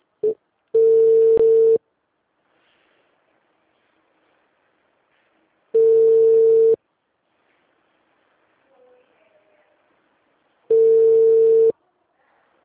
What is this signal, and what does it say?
Ringback tone (China, 450 Hz).
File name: out- -2010- - - .6816